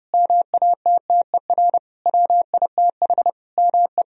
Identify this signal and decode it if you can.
Morse code, heard as a keyed tone, about 30 wpm, 700 Hz. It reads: MATTER WITH ME
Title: Sentences from Top 1000 Words - Review Part 4 - 30wpm